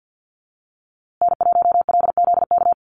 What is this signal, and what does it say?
Morse code, heard as a keyed tone, about 50 wpm, 700 Hz. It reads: N1LZK
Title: USA Call Signs Repeated 2x 50wpm